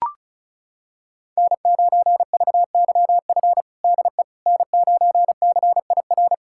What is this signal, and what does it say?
Morse code, heard as a keyed tone, about 34 wpm, 700 Hz. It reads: N9VYF DE N9CIR
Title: USA Callsign DE Callsign - 5 Characters - 35wpm